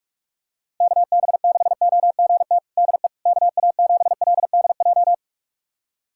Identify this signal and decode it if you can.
Morse code, heard as a keyed tone, about 46 wpm, 700 Hz. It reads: KB6OGT DE KA7LDJ
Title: USA Callsign DE Callsign - 6 Characters - 45wpm